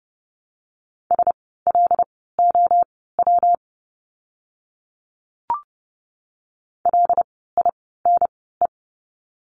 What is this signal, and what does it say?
Morse code, heard as a keyed tone, about 30 wpm, spaced as for 17 wpm, 700 Hz. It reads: SLOW LINE
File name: course / 30:10wpm / 115 Lesson 19f - Letter w - Words 30wpm